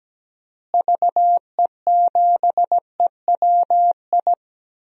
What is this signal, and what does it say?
Morse code, heard as a keyed tone, about 17 wpm, 700 Hz. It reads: VE7EWI
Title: DXCC-100 Call Signs - 6 Characters - Repeated 2x - 17wpm